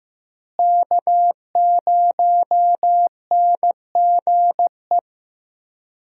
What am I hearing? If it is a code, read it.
Morse code, heard as a keyed tone, about 15 wpm, 700 Hz. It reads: K0NGE
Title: USA Call Signs - 5 Characters - Repeated 2x - 15wpm